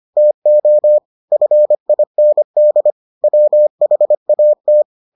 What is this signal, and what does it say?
Morse code, heard as a keyed tone, about 25 wpm, 600 Hz. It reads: TO FIND WHAT